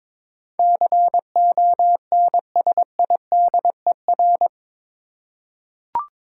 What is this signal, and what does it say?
Morse code, heard as a keyed tone, about 22 wpm, 700 Hz. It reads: CONSIDER
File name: practice / 22wpm / Eight Letter Words - 22wpm